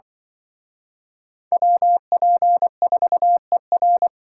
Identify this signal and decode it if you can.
Morse code, heard as a keyed tone, about 25 wpm, 700 Hz. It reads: WP4ER